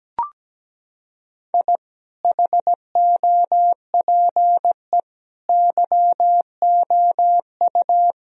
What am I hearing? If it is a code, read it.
Morse code, heard as a keyed tone, about 17 wpm, 700 Hz. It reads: I HOPE YOU
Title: Sentences from Top 1000 Words - Review Part 4 - 17wpm